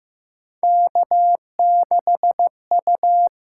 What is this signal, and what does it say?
Morse code, heard as a keyed tone, about 15 wpm, 700 Hz. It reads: K6U
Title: USA Call Signs - 4 Characters - 6x Character Spacing - 15wpm